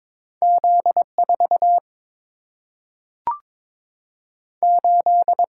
Morse code, heard as a keyed tone, about 22 wpm, 700 Hz. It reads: Z 4 8
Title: Single-Letter Single-Number 22wpm